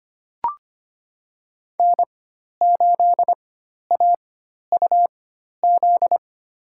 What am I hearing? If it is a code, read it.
Morse code, heard as a keyed tone, about 25 wpm, spaced as for 12 wpm, 700 Hz. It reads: N8AUZ
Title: USA Call Signs - 4x Character Spacing - 25wpm